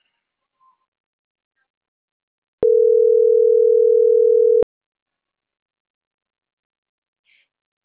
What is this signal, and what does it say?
Ringback tone (North America), 440 + 480 Hz.